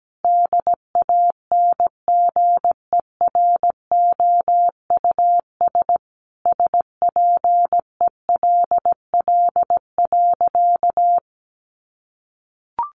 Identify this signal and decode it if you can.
Morse code, heard as a keyed tone, about 17 wpm, 700 Hz. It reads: DANGEROUS SPELL.
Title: Sentences from Top 1200 Words 17wpm